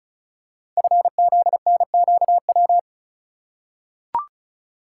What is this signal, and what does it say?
Morse code, heard as a keyed tone, about 36 wpm, 690 Hz. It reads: FZNQW
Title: Code Groups - 5 Letters - 35wpm